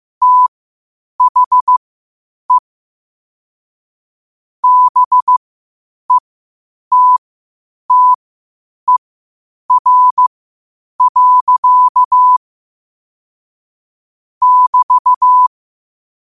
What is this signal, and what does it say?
Morse code, heard as a keyed tone, about 15 wpm, spaced as for 8 wpm, 1000 Hz. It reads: THE BETTER. <BT>
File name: qotdfiles / 8wpm